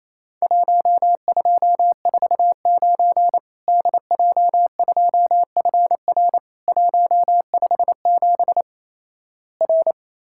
Morse code, heard as a keyed tone, about 28 wpm, 700 Hz. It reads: 1249 DJ2FR 157 R
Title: Worked All Europe DX Contest - QTC Exchange - 28wpm